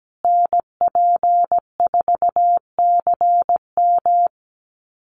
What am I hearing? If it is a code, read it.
Morse code, heard as a keyed tone, about 17 wpm, 700 Hz. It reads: NP4CM
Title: USA Call Signs Repeated 3x 17wpm